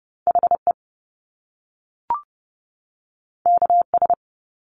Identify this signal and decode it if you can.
Morse code, heard as a keyed tone, about 30 wpm, 700 Hz. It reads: HE KS
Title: Top Two Letter Combinations in Words 30wpm